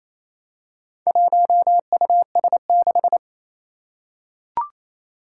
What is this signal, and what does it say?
Morse code, heard as a keyed tone, about 28 wpm, 700 Hz. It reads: 1US6
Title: Four Challenging Letter-Number Combinations 28wpm